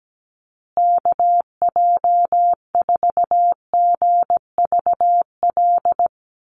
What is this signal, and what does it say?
Morse code, heard as a keyed tone, about 17 wpm, 700 Hz. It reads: KJ4GVL
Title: USA Call Signs Repeated 3x 17wpm